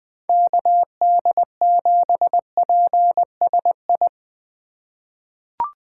Morse code, heard as a keyed tone, about 21 wpm, 700 Hz. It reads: KD7PSI
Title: USA Call Signs - 6 Characters - 2x Character Spacing - 20wpm